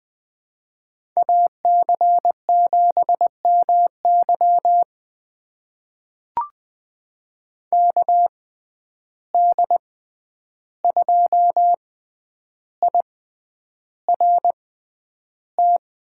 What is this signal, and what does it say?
Morse code, heard as a keyed tone, about 20 wpm, spaced as for 10 wpm, 700 Hz. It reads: AC7MY KD2IRT
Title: USA Call Signs - 6x Character Spacing - 20wpm